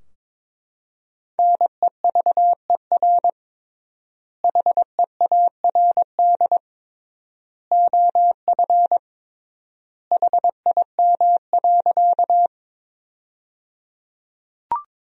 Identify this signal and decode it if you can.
Morse code, heard as a keyed tone, about 22 wpm, 700 Hz. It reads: NEVER HEARD OF HIM.
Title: 3x Word-Spacing - Sentences from Top 400 Words - 22wpm